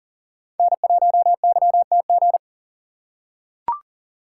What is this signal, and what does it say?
Morse code, heard as a keyed tone, about 40 wpm, 700 Hz. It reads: N1YTG
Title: USA Call Signs - 5 Characters - 3x Character Spacing - 40wpm